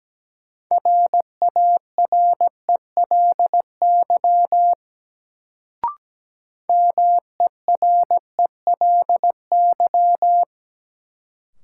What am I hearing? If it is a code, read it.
Morse code, heard as a keyed tone, about 17 wpm, 700 Hz. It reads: RARELY MERELY
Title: Sentences from Top 1400 Words - New Word Drill Alt - 17wpm